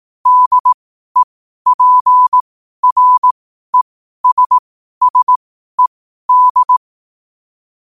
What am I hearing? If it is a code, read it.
Morse code, heard as a keyed tone, about 18 wpm, spaced as for 12 wpm, 1000 Hz. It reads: DEPRESSED
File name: qotdfiles / 12wpm